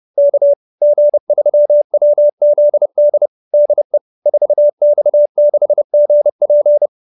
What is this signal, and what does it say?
Morse code, heard as a keyed tone, about 30 wpm, 590 Hz. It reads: K G3WZD DE 4X6GP
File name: QSO310 30